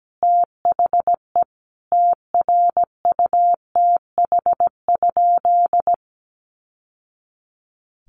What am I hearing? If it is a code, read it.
Morse code, heard as a keyed tone, about 17 wpm, 700 Hz. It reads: THE TRUTH?
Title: Sentences from Top 1000 Words 17wpm